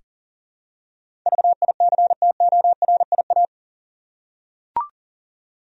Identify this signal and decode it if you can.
Morse code, heard as a keyed tone, about 40 wpm, 700 Hz. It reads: VICTORIA
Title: USA Common Names - 40wpm